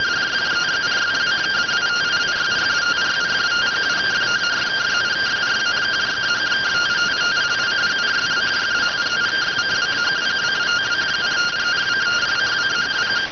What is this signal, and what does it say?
Two-tone teleprinter signal, about 170 Hz shift.